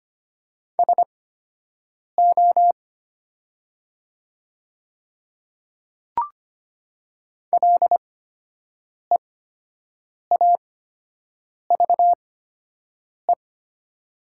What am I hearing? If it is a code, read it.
Morse code, heard as a keyed tone, about 25 wpm, spaced as for 7 wpm, 700 Hz. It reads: SO LEAVE
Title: 8x Character Spacing - Top 500 Words - 25wpm